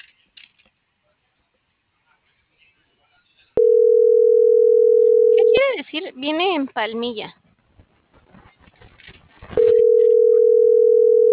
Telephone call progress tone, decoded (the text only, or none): ringback tone